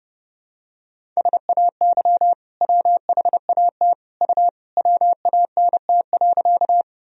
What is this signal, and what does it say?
Morse code, heard as a keyed tone, about 30 wpm, 700 Hz. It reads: SAY WHAT U WANT.
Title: Sentences from Top 200 Words and Abbreviations 30wpm